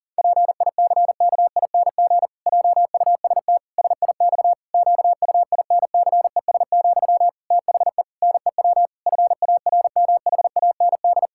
Morse code, heard as a keyed tone, about 40 wpm, 700 Hz. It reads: PICKING JUST SIX QUINCES, THE NEW FARMHAND
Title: Pangram Challenge v2 - 40wpm